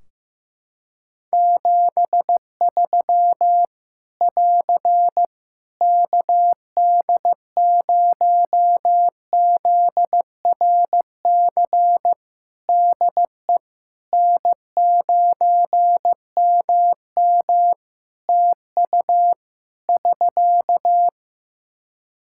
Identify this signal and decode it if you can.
Morse code, heard as a keyed tone, about 16 wpm, 700 Hz. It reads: 73 <AR> KD0ZRC DE N9MM TU <SK>
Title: Practice QSO - Exchange 4 - Ending - 15wpm